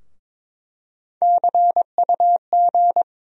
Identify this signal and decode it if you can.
Morse code, heard as a keyed tone, about 22 wpm, 700 Hz. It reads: CUG
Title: USA Call Sign Postfixes - 3 Letters 22wpm